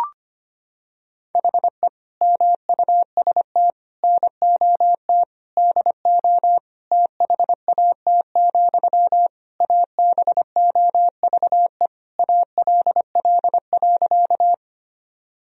Morse code, heard as a keyed tone, about 25 wpm, 700 Hz. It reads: HE MUST NOT DO THAT, ABOVE ALL.
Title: Sentences from Top 1000 Words - Review Part 2 - 25wpm